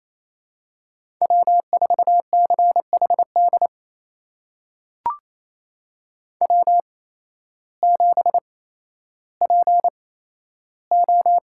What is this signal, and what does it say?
Morse code, heard as a keyed tone, about 28 wpm, spaced as for 13 wpm, 700 Hz. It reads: W4CHD W7PO
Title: USA Call Signs - 5 Characters - 8x Character Spacing - 28wpm